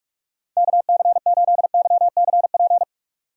Morse code, heard as a keyed tone, about 46 wpm, 700 Hz. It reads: KC8YCP
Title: USA Call Signs - 4x Character Spacing - 45wpm